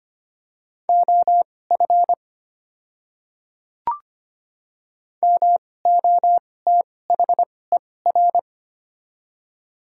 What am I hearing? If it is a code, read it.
Morse code, heard as a keyed tone, about 25 wpm, spaced as for 18 wpm, 700 Hz. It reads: OF MOTHER
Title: 2x Character Spacing - Top 200 Words - 25wpm